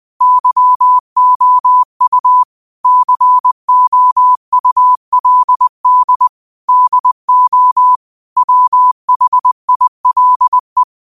Morse code, heard as a keyed tone, about 20 wpm, 1000 Hz. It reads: YOU COULD DO WHILE
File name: qotdfiles / 20wpm